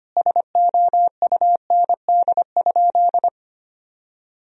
Morse code, heard as a keyed tone, about 25 wpm, 700 Hz. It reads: SOUND?
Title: Sentences from Top 1000 Words - Review Part 1 - 25wpm